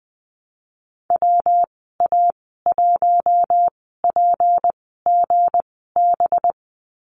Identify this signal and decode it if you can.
Morse code, heard as a keyed tone, about 20 wpm, spaced as for 15 wpm, 700 Hz. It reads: WA1PGB
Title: 193 Lesson 31g - Letter v - Call Signs 20wpm